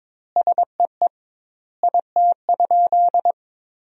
Morse code, heard as a keyed tone, about 22 wpm, 700 Hz. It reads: SEE IT?
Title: 2x Word-Spacing - Sentences from Top 100 Words - 22wpm